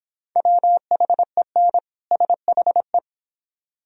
Morse code, heard as a keyed tone, about 26 wpm, 700 Hz. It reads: WHEN SHE